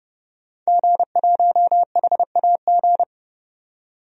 Morse code, heard as a keyed tone, about 30 wpm, 700 Hz. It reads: G1HAG
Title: DXCC-100 Call Signs - 5 Characters - 30wpm